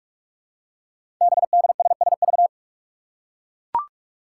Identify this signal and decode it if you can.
Morse code, heard as a keyed tone, about 44 wpm, 700 Hz. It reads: BDSSV
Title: Five Challenging Letter-Number Combinations 45wpm